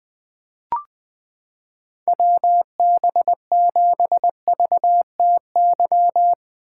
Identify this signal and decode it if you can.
Morse code, heard as a keyed tone, about 20 wpm, 700 Hz. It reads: WB7VTY
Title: 2023 SOTA Chaser Call Signs - 20wpm